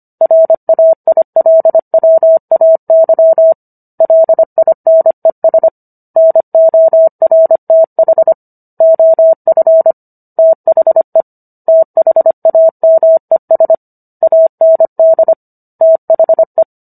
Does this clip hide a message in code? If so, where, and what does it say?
Yes: Morse code from 0.2 to 16.6 s, heard as a keyed tone, about 25 wpm, 650 Hz. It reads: RAILWAY LINES NORTH OF THE THAMES AND THE